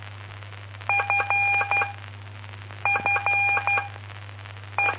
Morse code, heard as a keyed tone, about 12 wpm, 2400 Hz. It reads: F F E
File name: pp5bar / aulas de telegrafia / Aula-06